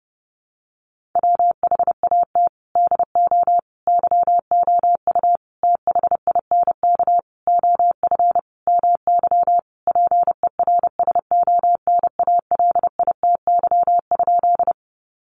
Morse code, heard as a keyed tone, about 30 wpm, 700 Hz. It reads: WHAT DO YOU THINK OF MY PERSONALITY?
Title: Sentences from Top 1400 Words 30wpm